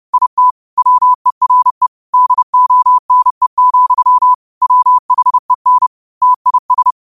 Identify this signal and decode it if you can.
Morse code, heard as a keyed tone, about 30 wpm, 1000 Hz. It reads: IT WERE DONE, WHEN TIS